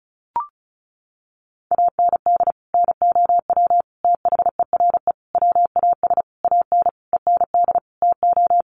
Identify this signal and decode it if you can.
Morse code, heard as a keyed tone, about 34 wpm, 700 Hz. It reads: AND NOW THERE WAS AN END TO
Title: Sentences from Top 1400 Words 35wpm